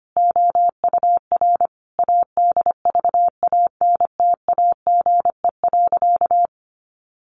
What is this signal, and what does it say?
Morse code, heard as a keyed tone, about 25 wpm, 700 Hz. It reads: OUR ADVANTAGE.
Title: Sentences from Top 1400 Words 25wpm